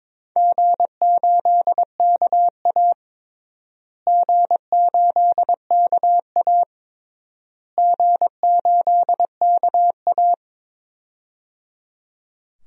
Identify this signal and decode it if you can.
Morse code, heard as a keyed tone, about 22 wpm, 700 Hz. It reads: G8KA G8KA G8KA
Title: DXCC-100 Call Signs - 4 Characters - Repeated 3x - 22wpm